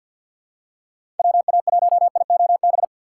Morse code, heard as a keyed tone, about 52 wpm, 700 Hz. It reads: WA1IOB
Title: USA Call Signs - 6 Characters - 6x Character Spacing - 50wpm